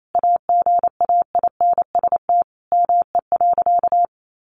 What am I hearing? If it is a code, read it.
Morse code, heard as a keyed tone, about 28 wpm, 700 Hz. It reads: AGAINST ME.
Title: Sentences from Top 1400 Words 28wpm